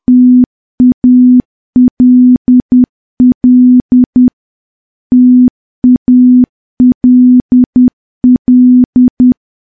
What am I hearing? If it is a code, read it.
Morse code, heard as a keyed tone, about 10 wpm, 250 Hz. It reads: TALL TALL
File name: morse / morse-training-10WPM